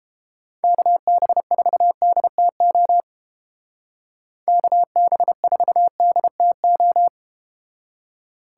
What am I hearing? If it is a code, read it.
Morse code, heard as a keyed tone, about 31 wpm, 700 Hz. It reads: KB4DTO KB4DTO